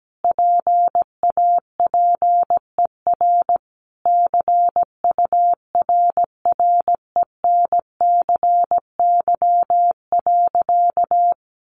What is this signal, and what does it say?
Morse code, heard as a keyed tone, about 17 wpm, 700 Hz. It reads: PAPER CURRENCY.
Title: Sentences from Top 1500 Words 17wpm